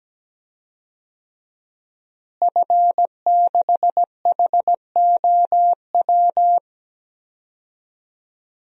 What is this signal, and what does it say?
Morse code, heard as a keyed tone, about 17 wpm, 700 Hz. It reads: F6HOW
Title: DXCC-100 Call Signs - 5 Characters - Repeated 2x - 17wpm